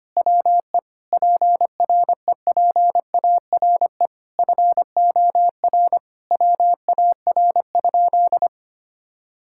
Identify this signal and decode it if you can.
Morse code, heard as a keyed tone, about 25 wpm, 700 Hz. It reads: WE PREPARE FOR WAR?
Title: Sentences from Top 1600 Words 25wpm